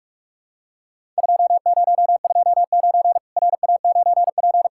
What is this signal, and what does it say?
Morse code, heard as a keyed tone, about 44 wpm, 700 Hz. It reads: 2029 RA9P
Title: Worked All Europe DX Contest - QTC Exchange - 45wpm